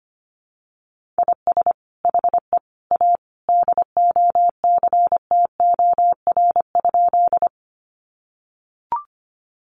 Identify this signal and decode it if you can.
Morse code, heard as a keyed tone, about 25 wpm, 700 Hz. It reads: IS HE A DOCTOR?
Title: Sentences from Top 1500 Words 25wpm